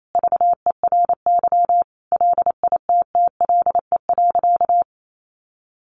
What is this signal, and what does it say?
Morse code, heard as a keyed tone, about 28 wpm, 700 Hz. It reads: VERY LITTLE.
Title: Sentences from Top 800 Words 28wpm